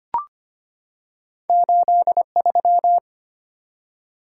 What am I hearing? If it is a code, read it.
Morse code, heard as a keyed tone, about 25 wpm, 700 Hz. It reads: 83
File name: Two Letter-Number Combinations - 25wpm